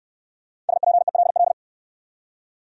Morse code, heard as a keyed tone, about 65 wpm, 700 Hz. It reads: SPELL